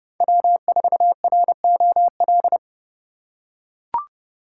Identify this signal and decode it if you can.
Morse code, heard as a keyed tone, about 26 wpm, 700 Hz. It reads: W4ROL